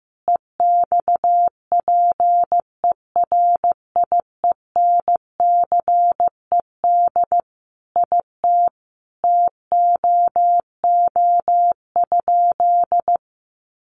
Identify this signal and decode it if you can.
Morse code, heard as a keyed tone, about 15 wpm, 700 Hz. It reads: EXPERIENCED IT TOO?
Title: Sentences from Top 1100 Words 15wpm